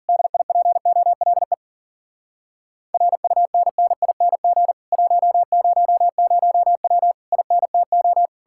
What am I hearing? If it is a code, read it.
Morse code, heard as a keyed tone, about 50 wpm, 700 Hz. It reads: DIPOLE RUNNING 100W INTO